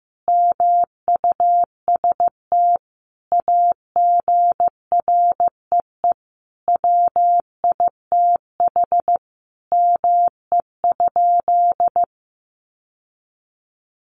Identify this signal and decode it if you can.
Morse code, heard as a keyed tone, about 15 wpm, 700 Hz. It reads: MUST AGREE WITH ME?
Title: Sentences from Top 1100 Words 15wpm